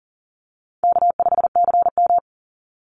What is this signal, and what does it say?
Morse code, heard as a keyed tone, about 40 wpm, 700 Hz. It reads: K5CM
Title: Call Signs Repeated 2x - 40wpm